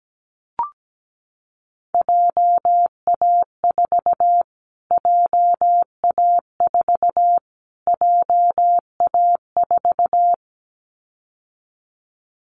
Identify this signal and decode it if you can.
Morse code, heard as a keyed tone, about 17 wpm, 700 Hz. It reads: JA4 JA4 JA4
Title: Callsign Prefixes for Top 25 DXCC - Repeated 3x - 17wpm